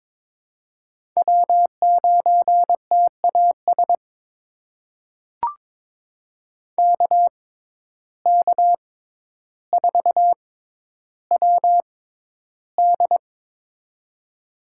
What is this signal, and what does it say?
Morse code, heard as a keyed tone, about 21 wpm, spaced as for 11 wpm, 700 Hz. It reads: W9TAS KK4WD
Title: USA Call Signs - 5 Characters - 6x Character Spacing - 22wpm